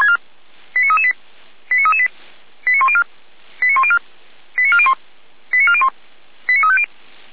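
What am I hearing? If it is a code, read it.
Five-tone selective call (ZVEI), heard as quick runs of five tones, tones of about 70 ms. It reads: 89338 89338 89104 89104 89401 89401 89350
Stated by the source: ZVEI five-tone call received on SDR.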